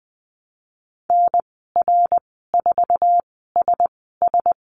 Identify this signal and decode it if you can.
Morse code, heard as a keyed tone, about 20 wpm, spaced as for 15 wpm, 700 Hz. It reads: NR4SS
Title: 082 Lesson 14g - Number 2 - Call Signs 20wpm